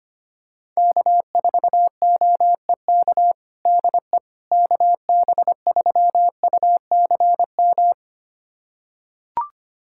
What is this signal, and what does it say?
Morse code, heard as a keyed tone, about 25 wpm, 700 Hz. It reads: K4OEK DE KB3UCM
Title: USA Callsign DE Callsign 25wpm